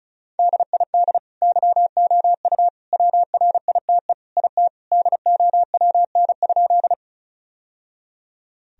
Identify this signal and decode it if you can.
Morse code, heard as a keyed tone, about 36 wpm, 700 Hz. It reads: DID YOU WRITE IT DOWN?
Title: Sentences from Top 1000 Words - Review Part 1 - 35wpm